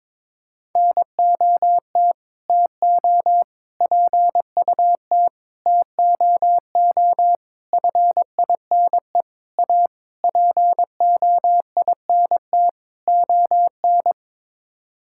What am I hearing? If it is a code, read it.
Morse code, heard as a keyed tone, about 22 wpm, 700 Hz. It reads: NOT TO PUT TOO FINE A POINT ON
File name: practice / 22wpm / Sets of 8 Words - 22wpm